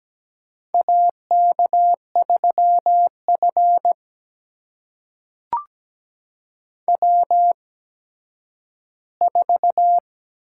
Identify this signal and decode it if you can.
Morse code, heard as a keyed tone, about 17 wpm, spaced as for 10 wpm, 700 Hz. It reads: AK3F W4
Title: USA Call Signs - 4 Characters - 8x Character Spacing - 17wpm